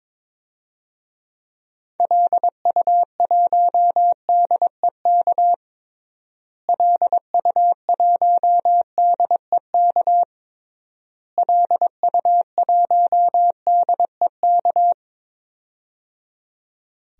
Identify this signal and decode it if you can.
Morse code, heard as a keyed tone, about 22 wpm, 700 Hz. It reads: LU1DEK LU1DEK LU1DEK
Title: DXCC-100 Call Signs - 6 Characters - Repeated 3x - 22wpm